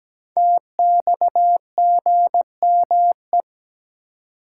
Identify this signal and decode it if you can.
Morse code, heard as a keyed tone, about 17 wpm, 700 Hz. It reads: TXGME
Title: Code Groups - 5 Letter:Numbers - 17wpm